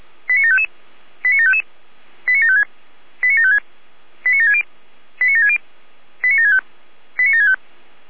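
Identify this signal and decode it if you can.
Five-tone selective call (ZVEI), heard as quick runs of five tones, tones of about 70 ms. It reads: 89755 89755 89756 89756 89760 89760 89765 89765
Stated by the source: ZVEI five-tone call received on SDR.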